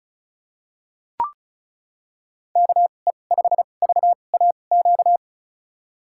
Morse code, heard as a keyed tone, about 36 wpm, spaced as for 25 wpm, 700 Hz. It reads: KE5VAQ
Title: USA Call Signs - 6 Characters - 2x Character Spacing - 35wpm